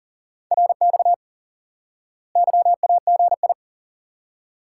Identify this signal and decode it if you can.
Morse code, heard as a keyed tone, about 40 wpm, 700 Hz. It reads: RX YAGI
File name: course / instant-qso-element-course-3 / IQE Course 3 - Lesson 89 - YAGI - Two Elements - 40wpm